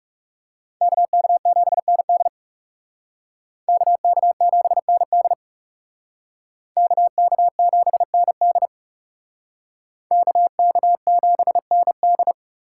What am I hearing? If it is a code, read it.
Morse code, heard as a keyed tone, about 38 wpm, 700 Hz. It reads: KK7ND KK7ND KK7ND KK7ND